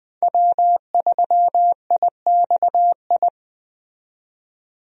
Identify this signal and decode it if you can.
Morse code, heard as a keyed tone, about 20 wpm, 700 Hz. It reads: W3IXI
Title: USA Call Signs - 5 Characters - 20wpm